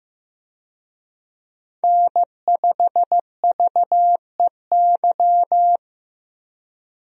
Morse code, heard as a keyed tone, about 15 wpm, 700 Hz. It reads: N5VEY
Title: USA Call Signs - 5 Characters - Repeated 3x - 15wpm